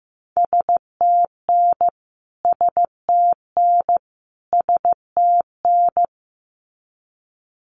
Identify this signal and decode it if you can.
Morse code, heard as a keyed tone, about 15 wpm, 700 Hz. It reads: STN STN STN
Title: Abbreviations Repeated 3x - 15wpm